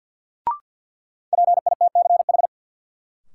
Morse code, heard as a keyed tone, about 51 wpm, 700 Hz. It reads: PITCH
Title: Sentences from Top 1300 Words - New Word Drill - 50wpm